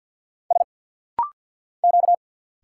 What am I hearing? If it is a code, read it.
Morse code, heard as a keyed tone, about 49 wpm, 690 Hz. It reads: S Q
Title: Single Letters - Rapid-Fire 50wpm